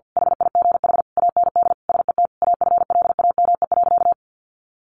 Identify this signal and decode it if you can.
Morse code, heard as a keyed tone, about 50 wpm, 700 Hz. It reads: HIGH AND SET AFLAME.
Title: A Sound Of Thunder 50wpm